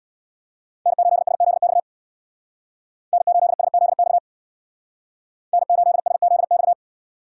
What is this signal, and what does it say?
Morse code, heard as a keyed tone, about 65 wpm, 700 Hz. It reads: N8SZX N8SZX N8SZX